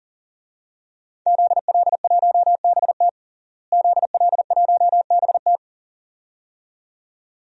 Morse code, heard as a keyed tone, about 37 wpm, 700 Hz. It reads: ZL1BT ZL1BT
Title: DXCC-100 Call Signs - 5 Characters - Repeated 2x - 40wpm